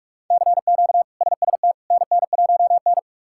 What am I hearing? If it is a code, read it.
Morse code, heard as a keyed tone, about 46 wpm, 700 Hz. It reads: CQ SST NN1N
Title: K1USN SST QSO Practice - 45wpm